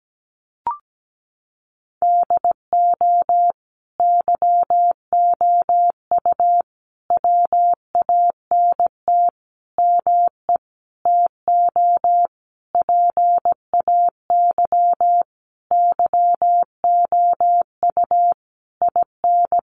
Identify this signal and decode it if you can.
Morse code, heard as a keyed tone, about 17 wpm, 700 Hz. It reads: DO YOU WANT ME TO PAY YOU IN
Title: Sentences from Top 1400 Words 17wpm